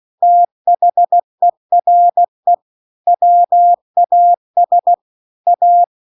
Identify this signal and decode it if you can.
Morse code, heard as a keyed tone, about 16 wpm, 700 Hz. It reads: THERE WAS A